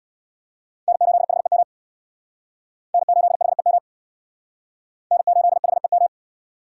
Morse code, heard as a keyed tone, about 60 wpm, 700 Hz. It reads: N8HEM N8HEM N8HEM
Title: Speed Racing - USA Call Signs 50wpm